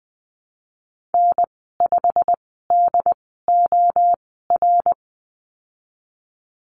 Morse code, heard as a keyed tone, about 20 wpm, spaced as for 15 wpm, 700 Hz. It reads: N5DOR
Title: 122 Lesson 20g - Number 3 - Call Signs 20wpm